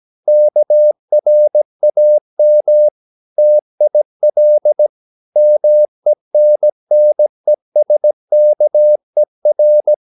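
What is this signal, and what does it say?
Morse code, heard as a keyed tone, about 17 wpm, 600 Hz. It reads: KRAM TIL MENNESKER
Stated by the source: Slow (QRS) Morse practice at 17 wpm, from CW med Gnister.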